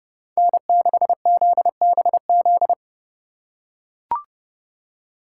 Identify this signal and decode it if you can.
Morse code, heard as a keyed tone, about 29 wpm, 700 Hz. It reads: N6ZBZ